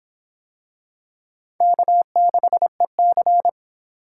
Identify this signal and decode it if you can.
Morse code, heard as a keyed tone, about 25 wpm, 700 Hz. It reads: K6EC